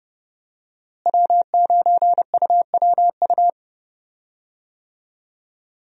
Morse code, heard as a keyed tone, about 31 wpm, 700 Hz. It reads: W9UWU
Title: USA Call Signs - 5 Characters - Repeated 2x - 30wpm